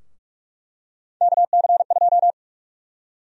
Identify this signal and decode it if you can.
Morse code, heard as a keyed tone, about 46 wpm, 700 Hz. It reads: KC2
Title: USA Call Sign Prefixes - Repeated 2x - 45wpm